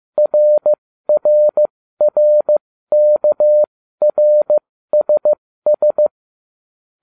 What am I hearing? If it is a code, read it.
Morse code, heard as a keyed tone, about 15 wpm, spaced as for 13 wpm, 600 Hz. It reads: RRRKRSS